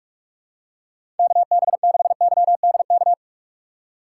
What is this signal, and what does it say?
Morse code, heard as a keyed tone, about 47 wpm, 700 Hz. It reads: KB6YDK
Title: USA Call Signs - 6 Characters - 6x Character Spacing - 45wpm